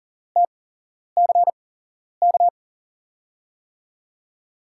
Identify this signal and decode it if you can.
Morse code, heard as a keyed tone, about 38 wpm, spaced as for 11 wpm, 700 Hz. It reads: TCK
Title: USA Call Signs - 6 Characters - 8x Character Spacing - 40wpm